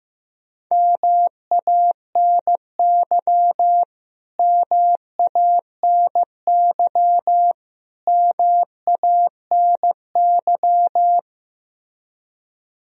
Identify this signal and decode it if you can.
Morse code, heard as a keyed tone, about 15 wpm, 700 Hz. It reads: MANY MANY MANY
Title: Top 100 Words Repeated 15wpm